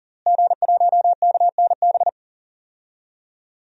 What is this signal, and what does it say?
Morse code, heard as a keyed tone, about 40 wpm, 700 Hz. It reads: G1KNB
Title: DXCC-100 Call Signs - 5 Characters - Repeated 2x - 40wpm